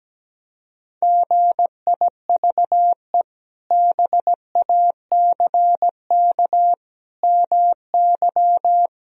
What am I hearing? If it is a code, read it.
Morse code, heard as a keyed tone, about 17 wpm, 700 Hz. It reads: GIVE BACK MY
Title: Sentences from Top 1000 Words - Review Part 1 - 17wpm